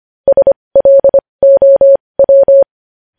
Morse code, heard as a keyed tone, about 25 wpm, spaced as for 20 wpm, 550 Hz. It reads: SLOW
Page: News Headlines in Morse Code at 20 WPM